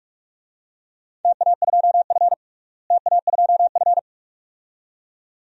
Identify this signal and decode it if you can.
Morse code, heard as a keyed tone, about 46 wpm, 700 Hz. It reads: TA2F TA2F
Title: DXCC-100 Call Signs - 4 Characters - Repeated 2x - 45wpm